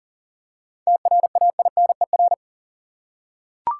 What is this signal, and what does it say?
Morse code, heard as a keyed tone, about 40 wpm, 700 Hz. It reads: TRAINER
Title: Seven Letter Words - 40wpm